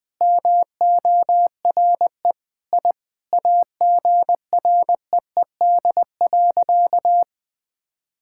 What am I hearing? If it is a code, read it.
Morse code, heard as a keyed tone, about 20 wpm, 700 Hz. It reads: MORE I AGREED.
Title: Sentences with 8-Word Phrases - 20wpm